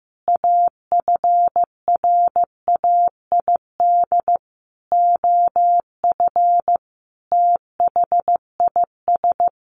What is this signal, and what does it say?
Morse code, heard as a keyed tone, about 15 wpm, 700 Hz. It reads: AFRAID OF THIS
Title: Sentences from Top 1500 Words 15wpm